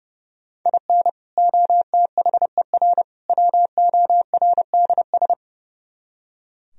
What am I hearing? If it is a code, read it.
Morse code, heard as a keyed tone, about 38 wpm, 700 Hz. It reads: IN OTHER WORDS